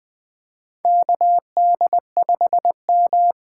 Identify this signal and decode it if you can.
Morse code, heard as a keyed tone, about 20 wpm, 700 Hz. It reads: KD5M
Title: USA Call Signs - 6 Characters - 8x Character Spacing - 20wpm